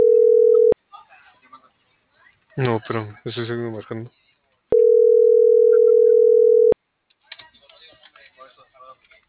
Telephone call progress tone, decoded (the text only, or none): ringback tone